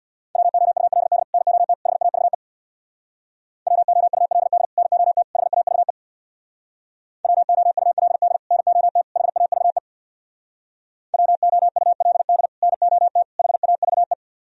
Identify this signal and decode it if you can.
Morse code, heard as a keyed tone, about 64 wpm, 700 Hz. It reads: WOULD NOT HAVE WOULD NOT HAVE WOULD NOT HAVE WOULD NOT HAVE